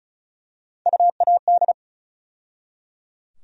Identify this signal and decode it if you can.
Morse code, heard as a keyed tone, about 34 wpm, 700 Hz. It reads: UAD
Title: USA Call Sign Postfixes - 3 Letters 35wpm